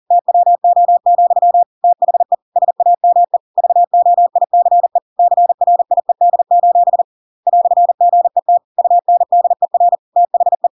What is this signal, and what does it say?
Morse code, heard as a keyed tone, about 40 wpm, 700 Hz. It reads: TWO, THE SAME VOICE CRIED: "GET UNDER THE
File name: Pinocchio / Pinocchio ch6 40 40 ik1hge